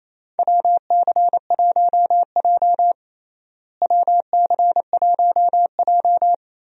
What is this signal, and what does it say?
Morse code, heard as a keyed tone, about 28 wpm, 700 Hz. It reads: WC1J WC1J
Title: USA Call Signs - 4 Characters - Repeated 3x - 28wpm